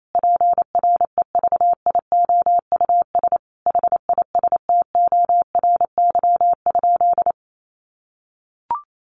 Morse code, heard as a keyed tone, about 28 wpm, 700 Hz. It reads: PREVIOUS HISTORY?
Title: Sentences from Top 1500 Words 28wpm